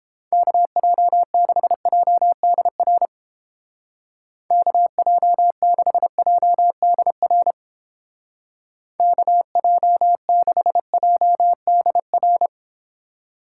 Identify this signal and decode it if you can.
Morse code, heard as a keyed tone, about 31 wpm, 700 Hz. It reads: KJ6JDR KJ6JDR KJ6JDR